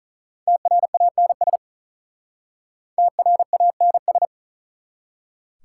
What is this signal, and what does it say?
Morse code, heard as a keyed tone, about 44 wpm, 700 Hz. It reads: TRANS TRANS